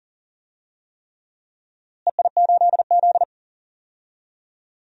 Morse code, heard as a keyed tone, about 37 wpm, 700 Hz. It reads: EI8Z
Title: DXCC-100 Call Signs - 4 Characters - Repeated 3x - 40wpm